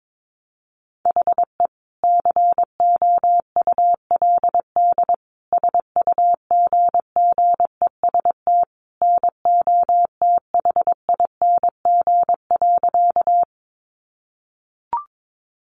Morse code, heard as a keyed tone, about 22 wpm, 700 Hz. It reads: HE COULD SUGGEST NOTHING.
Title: Sentences from Top 1300 Words 22wpm